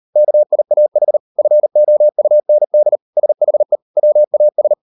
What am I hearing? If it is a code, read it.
Morse code, heard as a keyed tone, about 38 wpm, 600 Hz. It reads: KIAH FOUND SHE WAS